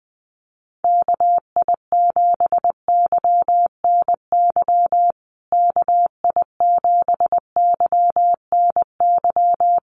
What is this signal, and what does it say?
Morse code, heard as a keyed tone, about 20 wpm, 700 Hz. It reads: KI7YNY KI7YNY
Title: USA Call Signs Repeated 3x 20wpm